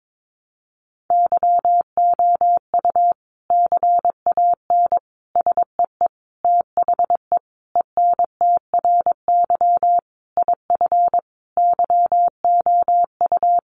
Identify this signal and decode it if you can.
Morse code, heard as a keyed tone, about 22 wpm, 700 Hz. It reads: YOU CAN SEE THE ENTRY IF YOU
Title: Sentences from Top 1500 Words 22wpm